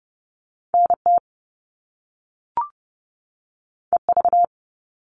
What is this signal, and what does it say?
Morse code, heard as a keyed tone, about 30 wpm, 700 Hz. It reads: NT EV
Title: All Cut Numbers - 2-Digit 30wpm